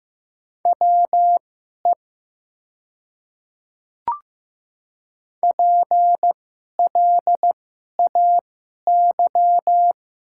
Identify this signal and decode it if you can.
Morse code, heard as a keyed tone, about 15 wpm, spaced as for 11 wpm, 700 Hz. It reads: WE PLAY
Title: 2x Character Spacing - Top 500 Words - 15wpm